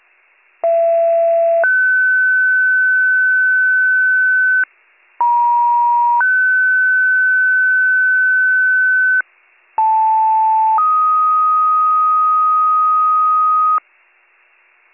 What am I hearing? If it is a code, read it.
Two-tone page, heard as a one-second tone, then a three-second tone, about 670 then 1500 Hz, about 950 then 1500 Hz, about 880 then 1250 Hz, with noise. It is heard from 0.6 to 13.8 s.